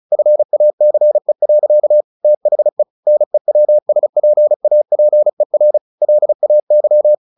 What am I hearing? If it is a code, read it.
Morse code, heard as a keyed tone, about 34 wpm, 600 Hz. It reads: FACE. THE NEWSPAPER LAY
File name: the old man and the sea / the old man and the sea 1 35wpm